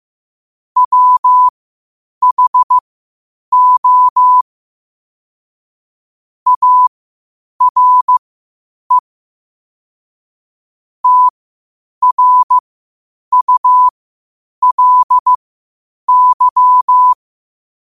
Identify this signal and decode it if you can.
Morse code, heard as a keyed tone, about 15 wpm, spaced as for 8 wpm, 1000 Hz. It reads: WHO ARE TRULY
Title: 8 WPM morse code quotes for Fri, 01 Mar 2024